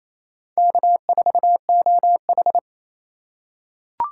Morse code, heard as a keyed tone, about 28 wpm, 700 Hz. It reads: K4OH
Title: USA Call Signs - 4 Characters - 8x Character Spacing - 28wpm